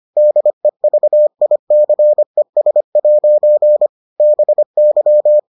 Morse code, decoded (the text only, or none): DEVICES' BY